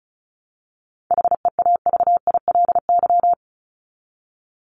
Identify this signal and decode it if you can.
Morse code, heard as a keyed tone, about 34 wpm, 700 Hz. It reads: HEAVILY
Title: LY Suffix - Top 100 Words with Base Words 35wpm